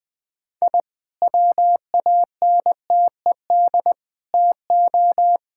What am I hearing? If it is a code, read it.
Morse code, heard as a keyed tone, about 20 wpm, 700 Hz. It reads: I WANTED TO
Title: Sentences with 3-Word Phrases - Spoken Beginning - 20wpm